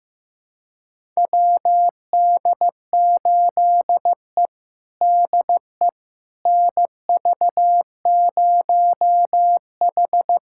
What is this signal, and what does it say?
Morse code, heard as a keyed tone, about 15 wpm, 700 Hz. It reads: WD8E DE NV0H
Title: USA Callsign DE Callsign - 4 Characters - 15wpm